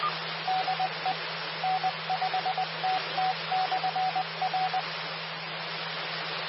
Morse code, heard as a keyed tone, about 20 wpm, 750 Hz. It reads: DE N5TT/R
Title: Austin, TX Amateur Repeaters